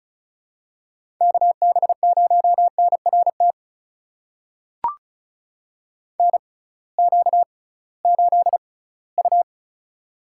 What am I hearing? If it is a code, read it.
Morse code, heard as a keyed tone, about 36 wpm, spaced as for 20 wpm, 700 Hz. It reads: KB0NRT NQ8U
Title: USA Call Signs - 6x Character Spacing - 35wpm